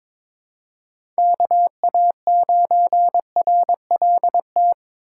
Morse code, heard as a keyed tone, about 22 wpm, 700 Hz. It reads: KA9RLT
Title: USA Call Signs - 6 Characters - 4x Character Spacing - 22wpm